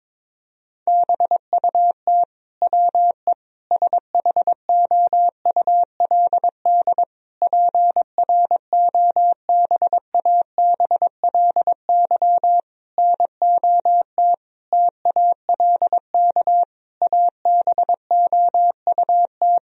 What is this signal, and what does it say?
Morse code, heard as a keyed tone, about 22 wpm, 700 Hz. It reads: BUT WE SHOULD PROBABLY NOT TALK ABOUT
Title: Sentences from Top 1000 Words - Review Part 4 - 22wpm